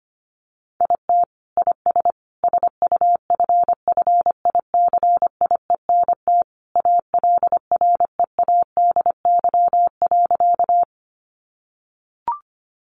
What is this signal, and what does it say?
Morse code, heard as a keyed tone, about 25 wpm, 700 Hz. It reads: IT IS SUFFICIENT ALREADY.
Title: Sentences from Top 1400 Words 25wpm